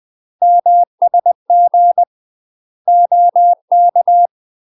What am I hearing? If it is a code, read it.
Morse code, heard as a keyed tone, about 20 wpm, 700 Hz. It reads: MSG OK
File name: UK MASTER QSOs TRAINING / 20WPM / UK MASTER QSOs 0022